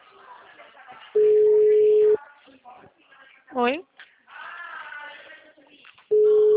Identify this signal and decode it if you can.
Ringback tone (Europe, 425 Hz), with speech.